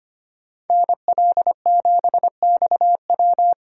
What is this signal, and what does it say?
Morse code, heard as a keyed tone, about 33 wpm, 700 Hz. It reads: NL7XW